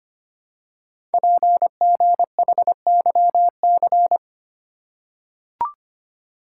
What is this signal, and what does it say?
Morse code, heard as a keyed tone, about 25 wpm, 700 Hz. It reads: PGHYC
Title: Code Groups - 5 Letters - 25wpm